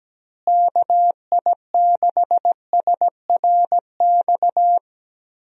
Morse code, heard as a keyed tone, about 17 wpm, 700 Hz. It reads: KI6SRX
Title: USA Call Signs - 6 Characters - Repeated 2x - 17wpm